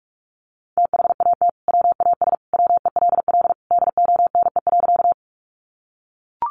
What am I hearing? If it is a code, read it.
Morse code, heard as a keyed tone, about 44 wpm, 700 Hz. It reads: THAT WAS WELL DONE.
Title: Sentences from Top 500 Words 45wpm